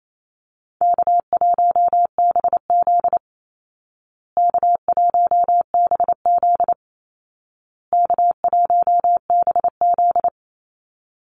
Morse code, heard as a keyed tone, about 28 wpm, 700 Hz. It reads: K1BZ K1BZ K1BZ
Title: Call Signs Repeated 2x - 28wpm